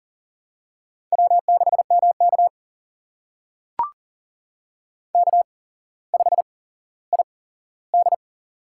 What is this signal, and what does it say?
Morse code, heard as a keyed tone, about 40 wpm, spaced as for 17 wpm, 700 Hz. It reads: W6MK K5ID
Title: USA Call Signs - 4 Characters - 8x Character Spacing - 40wpm